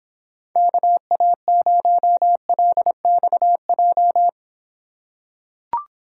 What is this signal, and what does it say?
Morse code, heard as a keyed tone, about 25 wpm, 700 Hz. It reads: KA0LXJ